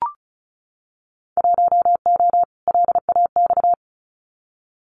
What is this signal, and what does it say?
Morse code, heard as a keyed tone, about 34 wpm, 700 Hz. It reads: 1O LAX
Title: Winter Field Day Exchanges 35wpm